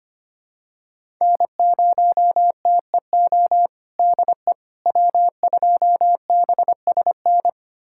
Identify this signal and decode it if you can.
Morse code, heard as a keyed tone, about 25 wpm, 700 Hz. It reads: N0TEO DE W2BSN
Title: USA Callsign DE Callsign - 5 Characters - 25wpm